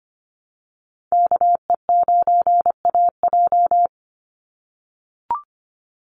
Morse code, heard as a keyed tone, about 25 wpm, 700 Hz. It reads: KE9AJ
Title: NA SOTA Chaser Callsigns 25wpm